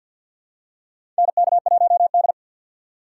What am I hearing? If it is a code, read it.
Morse code, heard as a keyed tone, about 48 wpm, 700 Hz. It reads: NK1D
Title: USA Call Signs - 4 Characters - 6x Character Spacing - 50wpm